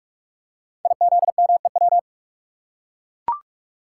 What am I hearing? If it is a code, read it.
Morse code, heard as a keyed tone, about 44 wpm, 700 Hz. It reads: IZMEW
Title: Code Groups - 5 Letters - 45wpm